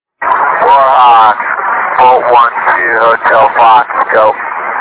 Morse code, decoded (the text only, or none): none